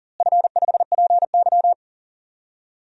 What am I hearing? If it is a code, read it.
Morse code, heard as a keyed tone, about 40 wpm, 700 Hz. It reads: F5PY
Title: DXCC-100 Call Signs - 4 Characters - Repeated 3x - 40wpm